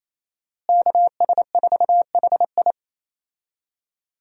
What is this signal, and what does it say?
Morse code, heard as a keyed tone, about 28 wpm, 700 Hz. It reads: KS4HI
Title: USA Call Signs - 5 Characters - Repeated 3x - 28wpm